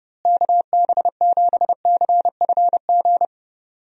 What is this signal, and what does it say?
Morse code, heard as a keyed tone, about 32 wpm, 700 Hz. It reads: KB7CFG
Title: USA Call Signs - 6 Characters - Repeated 3x - 30wpm